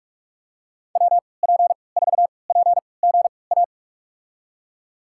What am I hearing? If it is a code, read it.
Morse code, heard as a keyed tone, about 43 wpm, spaced as for 25 wpm, 700 Hz. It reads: WP4PGA
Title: USA Call Signs - 6 Characters - 3x Character Spacing - 45wpm